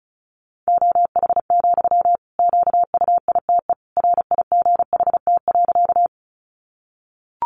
Morse code, heard as a keyed tone, about 34 wpm, 700 Hz. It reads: OH, QUITE RIGHT.
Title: Sentences from Top 700 Words 35wpm